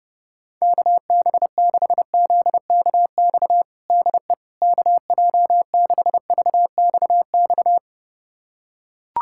Morse code, heard as a keyed tone, about 30 wpm, 700 Hz. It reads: KB6ZKX DE KJ6VXX
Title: USA Callsign DE Callsign - 6 Characters - 30wpm